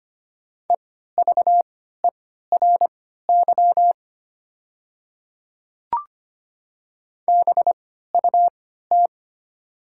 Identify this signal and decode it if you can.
Morse code, heard as a keyed tone, about 25 wpm, spaced as for 14 wpm, 690 Hz. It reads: EVERY BUT
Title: 3x Character Spacing - Top 300 Words - 25wpm